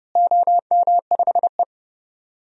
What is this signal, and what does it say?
Morse code, heard as a keyed tone, about 30 wpm, 700 Hz. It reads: OM5E
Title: DXCC-100 Call Signs - 4 Characters - Repeated 2x - 30wpm